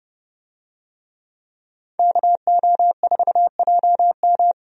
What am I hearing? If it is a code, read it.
Morse code, heard as a keyed tone, about 30 wpm, 700 Hz. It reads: KO4JM
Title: USA Call Signs - 5 Characters - Repeated 2x - 30wpm